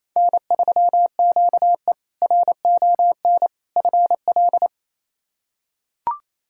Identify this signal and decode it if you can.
Morse code, heard as a keyed tone, about 28 wpm, 700 Hz. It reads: N3QE RON FL
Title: K1USN SST QSO Practice - 28wpm